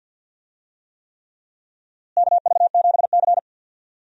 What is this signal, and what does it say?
Morse code, heard as a keyed tone, about 52 wpm, 700 Hz. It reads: KV7C
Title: USA Call Signs - 4 Characters - Repeated 3x - 50wpm